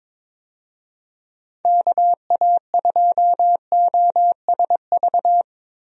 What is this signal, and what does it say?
Morse code, heard as a keyed tone, about 22 wpm, 700 Hz. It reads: KA2OSV
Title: USA Call Signs - 6 Characters - Repeated 2x - 22wpm